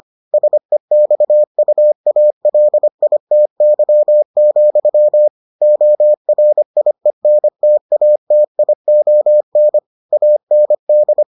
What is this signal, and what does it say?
Morse code, heard as a keyed tone, about 25 wpm, 600 Hz. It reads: SEXUALITY, ORIENTATION AND